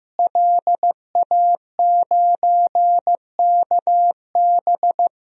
Morse code, heard as a keyed tone, about 16 wpm, 700 Hz. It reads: LA9KB
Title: DXCC-100 Call Signs - 5 Characters - Repeated 3x - 15wpm